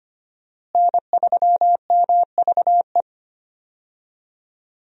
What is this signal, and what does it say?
Morse code, heard as a keyed tone, about 25 wpm, 700 Hz. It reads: N3MVE
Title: USA Call Signs - 5 Characters - Repeated 3x - 25wpm